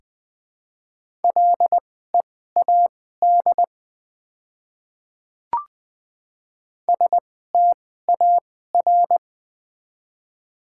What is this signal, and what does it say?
Morse code, heard as a keyed tone, about 20 wpm, spaced as for 14 wpm, 700 Hz. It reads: LEAD STAR
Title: 2x Character Spacing - Top 500 Words - 20wpm